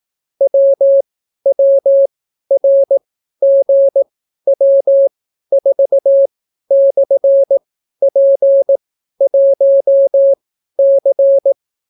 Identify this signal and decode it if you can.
Morse code, heard as a keyed tone, about 18 wpm, spaced as for 12 wpm, 550 Hz. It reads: WWRGW4/P1C